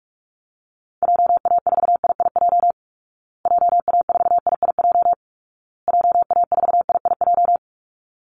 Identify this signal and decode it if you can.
Morse code, heard as a keyed tone, about 44 wpm, 700 Hz. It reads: JA4IIJ JA4IIJ JA4IIJ
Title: Call Signs Repeated 2x - 45wpm